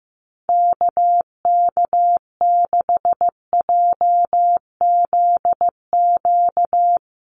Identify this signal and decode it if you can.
Morse code, heard as a keyed tone, about 15 wpm, 700 Hz. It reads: KK6JZQ
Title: USA Call Signs Repeated 2x 15wpm